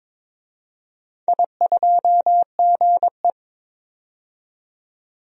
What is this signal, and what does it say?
Morse code, heard as a keyed tone, about 22 wpm, 700 Hz. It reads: I2GE
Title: DXCC-100 Call Signs - 4 Characters - 22wpm